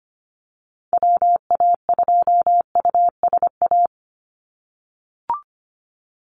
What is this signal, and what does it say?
Morse code, heard as a keyed tone, about 25 wpm, 700 Hz. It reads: WA2USA
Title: NA SOTA Activator Callsigns 25wpm